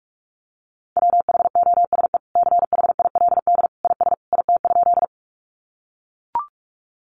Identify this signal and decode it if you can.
Morse code, heard as a keyed tone, about 44 wpm, 700 Hz. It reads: WHOSE CHILD IS IT?
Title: Sentences from Top 1200 Words 45wpm